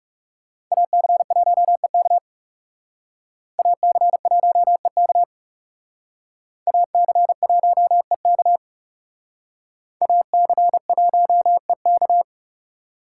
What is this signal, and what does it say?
Morse code, heard as a keyed tone, about 36 wpm, 700 Hz. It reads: AC1EK AC1EK AC1EK AC1EK
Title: Speed Racing - USA Call Signs 30wpm